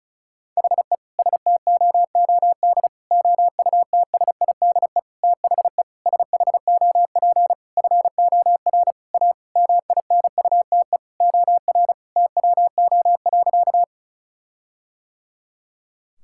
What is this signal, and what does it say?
Morse code, heard as a keyed tone, about 36 wpm, 700 Hz. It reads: HE STOOD OUTSIDE THE SHOP FOR A MINUTE OR TWO.
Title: Sentences from Top 1600 Words 35wpm